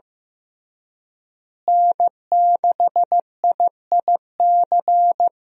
Morse code, heard as a keyed tone, about 15 wpm, 700 Hz. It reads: N6IIC
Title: USA Call Signs - 5 Characters - Repeated 3x - 15wpm